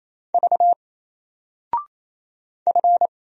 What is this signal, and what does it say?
Morse code, heard as a keyed tone, about 27 wpm, 700 Hz. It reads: V F
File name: Single-Letter-Number-28wpm